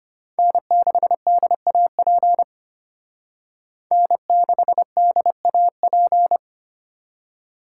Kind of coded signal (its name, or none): Morse code